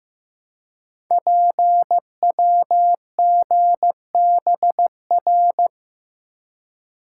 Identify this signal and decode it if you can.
Morse code, heard as a keyed tone, about 15 wpm, 690 Hz. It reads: PWGBR
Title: Code Groups - 5 Letters - 15wpm